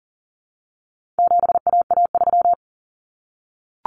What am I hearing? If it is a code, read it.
Morse code, heard as a keyed tone, about 40 wpm, 700 Hz. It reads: 7AA3
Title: Common Cut Numbers - 4-Digit 40wpm